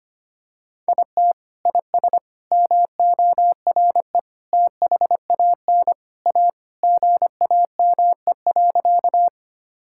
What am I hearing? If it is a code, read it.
Morse code, heard as a keyed tone, about 25 wpm, 700 Hz. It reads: IT IS MORE THAN A GAME.
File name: practice / 25wpm / Sentences with 3-Word Phrases - 25wpm